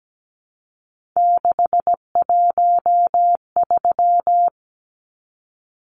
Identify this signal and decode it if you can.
Morse code, heard as a keyed tone, about 17 wpm, 700 Hz. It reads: 613
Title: Numbers - 3-Digit 17wpm